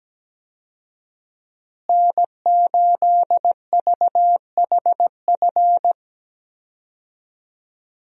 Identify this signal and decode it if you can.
Morse code, heard as a keyed tone, about 17 wpm, 700 Hz. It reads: N8VHF
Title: USA Call Signs - 5 Characters - Repeated 2x - 17wpm